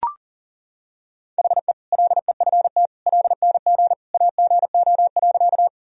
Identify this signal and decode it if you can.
Morse code, heard as a keyed tone, about 37 wpm, 700 Hz. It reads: HE LEFT LNG AGO.
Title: Sentences from Top 500 Words and Abbreviations 40wpm